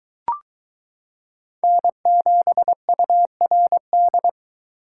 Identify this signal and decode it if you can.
Morse code, heard as a keyed tone, about 22 wpm, 700 Hz. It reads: N7URD